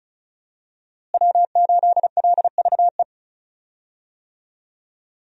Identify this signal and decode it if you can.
Morse code, heard as a keyed tone, about 36 wpm, 700 Hz. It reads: W8LVE
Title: USA Call Signs - 5 Characters - Repeated 3x - 35wpm